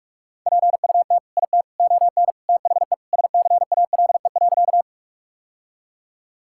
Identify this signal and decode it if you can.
Morse code, heard as a keyed tone, about 41 wpm, 700 Hz. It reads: PUT IT ON THE SCALE.
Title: Sentences from Top 1000 Words - Review Part 5 - 45wpm